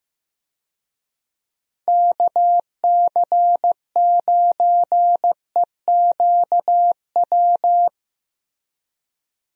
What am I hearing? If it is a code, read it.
Morse code, heard as a keyed tone, about 15 wpm, 700 Hz. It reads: KC9EQW
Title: USA Call Signs - 6 Characters - Repeated 2x - 15wpm